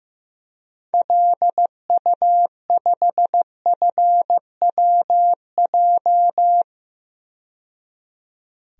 Morse code, heard as a keyed tone, about 15 wpm, 700 Hz. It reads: LU5FWJ
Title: DXCC-100 Call Signs - 6 Characters - 15wpm